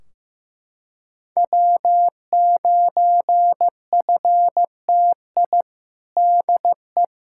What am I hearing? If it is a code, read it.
Morse code, heard as a keyed tone, about 15 wpm, 700 Hz. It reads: W9FTI DE
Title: K1USN SST QSO Practice 15wpm